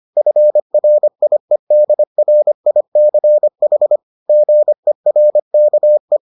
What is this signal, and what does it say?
Morse code, heard as a keyed tone, about 25 wpm, 600 Hz. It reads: FRIEDRICH GERKE